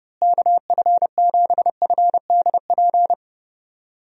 Morse code, heard as a keyed tone, about 30 wpm, 700 Hz. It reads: KF7FDP
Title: USA Call Signs - 6 Characters - Repeated 3x - 30wpm